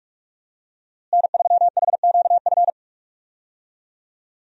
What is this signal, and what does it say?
Morse code, heard as a keyed tone, about 46 wpm, 700 Hz. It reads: N3HQF
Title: USA Call Signs - 5 Characters - Repeated 2x - 45wpm